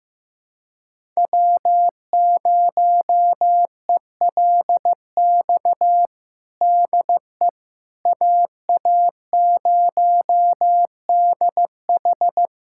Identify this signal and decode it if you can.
Morse code, heard as a keyed tone, about 15 wpm, 700 Hz. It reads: W0ELX DE AA0DH
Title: USA Callsign DE Callsign - 5 Characters - 15wpm